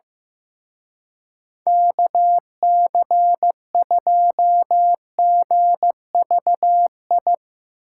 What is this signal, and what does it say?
Morse code, heard as a keyed tone, about 15 wpm, 700 Hz. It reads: KC2GVI
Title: Speed Racing - USA 2x3 Call Signs 15wpm